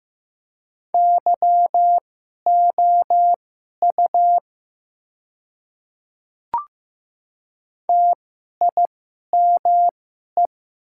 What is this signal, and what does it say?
Morse code, heard as a keyed tone, about 15 wpm, spaced as for 11 wpm, 690 Hz. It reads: YOU TIME
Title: 2x Character Spacing - Top 100 Words - 15wpm